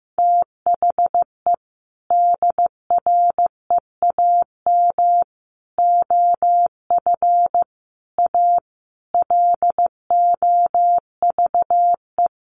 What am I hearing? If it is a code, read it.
Morse code, heard as a keyed tone, about 15 wpm, 700 Hz. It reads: THE DREAM OF A LOVE
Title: Sentences from Top 1500 Words 15wpm